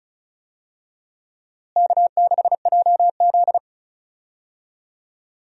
Morse code, heard as a keyed tone, about 34 wpm, 700 Hz. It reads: K6JZ
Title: USA Call Signs - 4 Characters - Repeated 2x - 35wpm